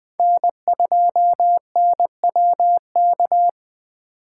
Morse code, heard as a keyed tone, about 20 wpm, 700 Hz. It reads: N2NWK
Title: 2023 POTA Activator Call Signs - 20wpm